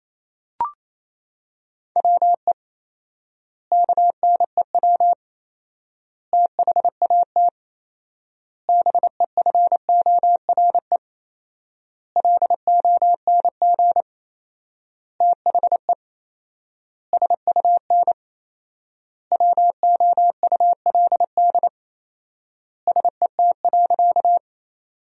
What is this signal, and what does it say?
Morse code, heard as a keyed tone, about 27 wpm, 700 Hz. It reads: WE KNEW THAT BEFORE LONG THE SUN WOULD SET.
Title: 4x Word-Spacing - Sentences from Top 400 Words - 28wpm